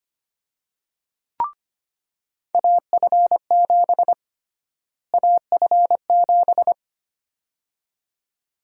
Morse code, heard as a keyed tone, about 25 wpm, 700 Hz. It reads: AF7 AF7
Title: USA Call Sign Prefixes - Repeated 2x - 25wpm